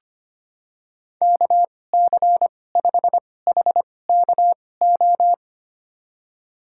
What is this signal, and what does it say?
Morse code, heard as a keyed tone, about 25 wpm, spaced as for 18 wpm, 690 Hz. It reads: KC5HKO
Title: USA Call Signs - 6 Characters - 2x Character Spacing - 25wpm